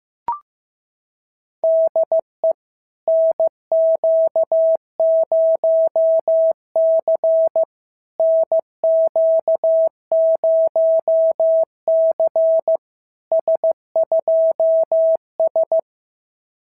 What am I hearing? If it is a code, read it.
Morse code, heard as a keyed tone, about 14 wpm, 650 Hz. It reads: DE NQ0C NQ0C S2S
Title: SOTA QSO Practice 15wpm